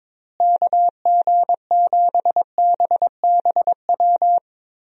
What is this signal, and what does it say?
Morse code, heard as a keyed tone, about 23 wpm, 700 Hz. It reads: KG7BBW
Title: USA Call Signs - 6 Characters - Repeated 3x - 22wpm